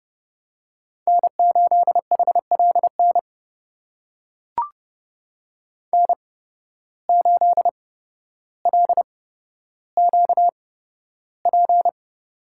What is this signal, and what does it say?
Morse code, heard as a keyed tone, about 30 wpm, spaced as for 13 wpm, 700 Hz. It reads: N8HLN N8LQP
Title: USA Call Signs - 5 Characters - 8x Character Spacing - 30wpm